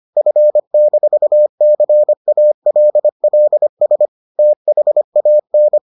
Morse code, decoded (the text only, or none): F-CALLS THAN